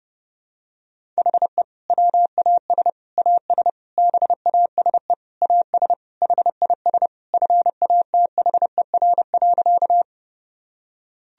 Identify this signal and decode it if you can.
Morse code, heard as a keyed tone, about 30 wpm, 700 Hz. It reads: HE WAS AS BASE AS HIS FATHER.
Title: Sentences from Top 1000 Words - Review Part 1 - 30wpm